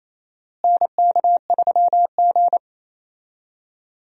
Morse code, heard as a keyed tone, about 28 wpm, 700 Hz. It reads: NK3G
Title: USA Call Signs - 4 Characters - Repeated 2x - 28wpm